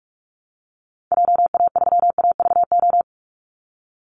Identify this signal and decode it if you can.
Morse code, heard as a keyed tone, about 44 wpm, 700 Hz. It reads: JA3AVO
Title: Call Signs Repeated 3x - 45wpm